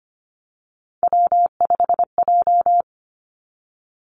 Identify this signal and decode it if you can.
Morse code, heard as a keyed tone, about 25 wpm, 700 Hz. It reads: W5J
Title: Three Challenging Letter-Number Combinations 25wpm